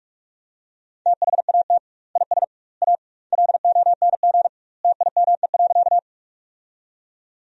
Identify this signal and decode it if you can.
Morse code, heard as a keyed tone, about 46 wpm, 700 Hz. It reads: THAT IS A LONG TIME.
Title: 2x Word-Spacing - Sentences from Top 100 Words - 45wpm